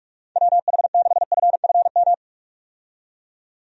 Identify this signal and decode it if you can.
Morse code, heard as a keyed tone, about 46 wpm, 700 Hz. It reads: WH6FFM
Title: USA Call Signs - 6 Characters - 45wpm